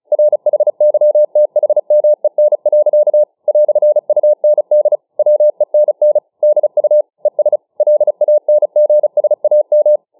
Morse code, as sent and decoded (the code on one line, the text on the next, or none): .-. .... -.-- - .... -- . -. .-.-.- / .-..-. ..- -. -.. / .-- . -. -. / -.. ..- / . ... / .-.. .- -. --. ... .- --
RHYTHMEN. "UND WENN DU ES LANGSAM